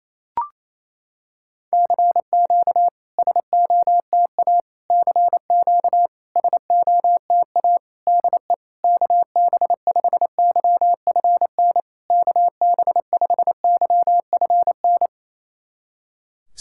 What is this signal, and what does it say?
Morse code, heard as a keyed tone, about 28 wpm, 700 Hz. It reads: CQ SOTA CQ SOTA DE KB5YFN KB5YFN
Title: SOTA QSO Practice 28wpm